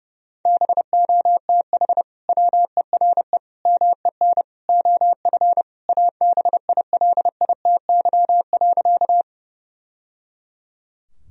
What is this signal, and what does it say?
Morse code, heard as a keyed tone, about 32 wpm, 700 Hz. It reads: BOTH WERE MEN OF ABILITY.
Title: Sentences from Top 1000 Words - Review Part 3 - 30wpm